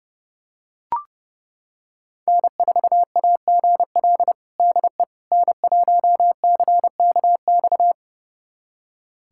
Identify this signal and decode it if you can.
Morse code, heard as a keyed tone, about 32 wpm, 700 Hz. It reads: N4AGL DE N1CKX
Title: USA Callsign DE Callsign - 5 Characters - 30wpm